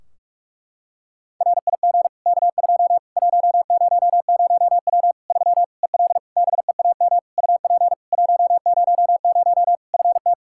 Morse code, heard as a keyed tone, about 46 wpm, 700 Hz. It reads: RIG K2 100W 3 EL BEAM UP 100 FT
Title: K1USN SST QSO Practice 45wpm